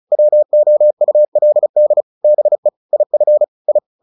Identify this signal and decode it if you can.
Morse code, heard as a keyed tone, about 34 wpm, 600 Hz. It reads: WOULD BE IF I
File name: the old man and the sea 2 35wpm